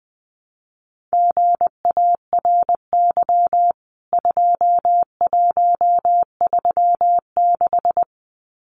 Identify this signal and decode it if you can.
Morse code, heard as a keyed tone, about 20 wpm, 700 Hz. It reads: GARY 2136
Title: CWT Exchange 20wpm